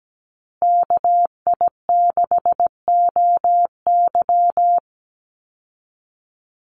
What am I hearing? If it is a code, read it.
Morse code, heard as a keyed tone, about 17 wpm, 700 Hz. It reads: KI6OY
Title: Call Signs Repeated 2x - 17wpm